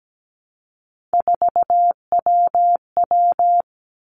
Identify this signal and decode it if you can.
Morse code, heard as a keyed tone, about 17 wpm, 700 Hz. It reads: 4WW
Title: Three Challenging Letter-Number Combinations 17wpm